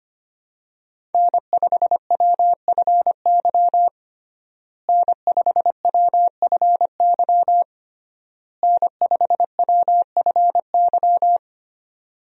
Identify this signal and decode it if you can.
Morse code, heard as a keyed tone, about 25 wpm, 700 Hz. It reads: N5WFY N5WFY N5WFY
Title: USA Call Signs - 5 Characters - Repeated 3x - 25wpm